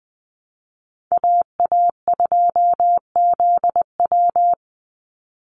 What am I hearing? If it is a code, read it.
Morse code, heard as a keyed tone, about 20 wpm, 700 Hz. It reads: AA2ZW
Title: Call Signs Repeated 2x - 20wpm